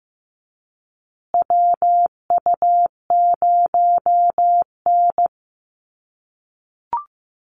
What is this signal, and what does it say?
Morse code, heard as a keyed tone, about 15 wpm, 700 Hz. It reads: WU0N
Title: USA Call Signs Repeated 2x 15wpm